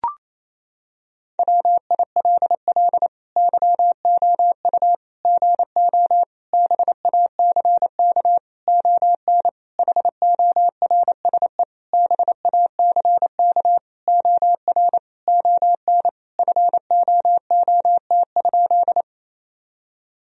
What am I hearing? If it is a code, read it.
Morse code, heard as a keyed tone, about 28 wpm, 700 Hz. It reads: WILL YOU GO BACK ON HORSE BACK OR ON FOOT?
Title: Sentences from Top 1000 Words - Review Part 3 - 28wpm